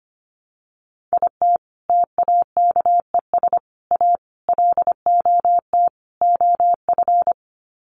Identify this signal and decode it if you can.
Morse code, heard as a keyed tone, about 25 wpm, 700 Hz. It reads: IT TAKES A LOT OF
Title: Sets of 5 Words Encore 25wpm